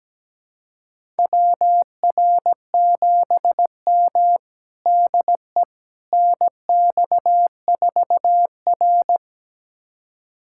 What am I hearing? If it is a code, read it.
Morse code, heard as a keyed tone, about 17 wpm, 700 Hz. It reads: WR7M DE NX4R
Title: USA Callsign DE Callsign - 4 Characters - 17wpm